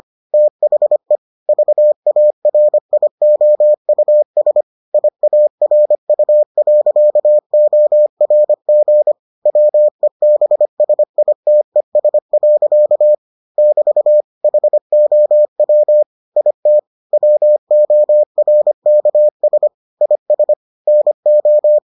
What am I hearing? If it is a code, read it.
Morse code, heard as a keyed tone, about 25 wpm, 600 Hz. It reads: THE VARIOUS IARU.ORG WEBSITES. <BT> HOW IT WORKS IS NO